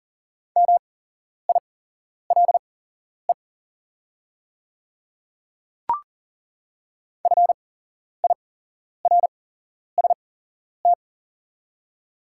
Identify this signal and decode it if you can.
Morse code, heard as a keyed tone, about 40 wpm, spaced as for 11 wpm, 700 Hz. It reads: MILE FIRST
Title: 8x Character Spacing - Top 300 Words - 40wpm